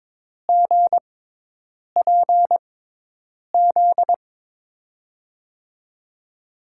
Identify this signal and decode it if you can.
Morse code, heard as a keyed tone, about 22 wpm, spaced as for 8 wpm, 700 Hz. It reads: GPZ
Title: USA Call Signs - 6x Character Spacing - 22wpm